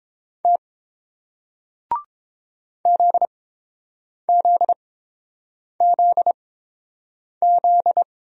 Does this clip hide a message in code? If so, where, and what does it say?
Yes: Morse code from 0.4 to 8.0 s, heard as a keyed tone, about 30 wpm, spaced as for 7 wpm, 690 Hz. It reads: T ZZZZ